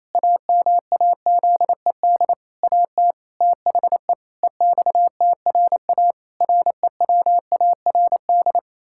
Morse code, heard as a keyed tone, about 27 wpm, 700 Hz. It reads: AMAZED AT THE EXTRA REWARD
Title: Pangram Challenge v2 - 28wpm